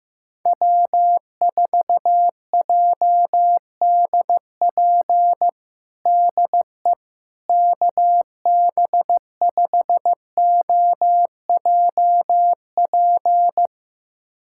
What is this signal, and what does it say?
Morse code, heard as a keyed tone, about 15 wpm, 700 Hz. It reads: W4JDP DE KB5OJP
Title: USA Callsign DE Callsign 15wpm